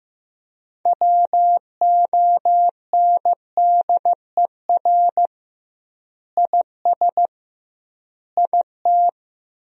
Morse code, heard as a keyed tone, about 16 wpm, 700 Hz. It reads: WONDER IS IT
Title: 2x Word-Spacing - Sentences from Top 500 Words - 15wpm